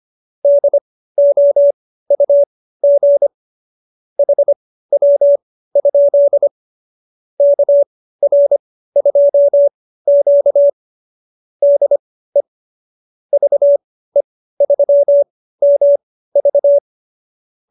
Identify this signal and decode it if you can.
Morse code, heard as a keyed tone, about 25 wpm, spaced as for 15 wpm, 570 Hz. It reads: DOUG HW? KR2Q DE VE3MV